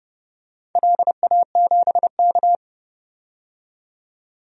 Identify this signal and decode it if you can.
Morse code, heard as a keyed tone, about 30 wpm, 700 Hz. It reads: LA7K
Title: DXCC-100 Call Signs - 4 Characters - 30wpm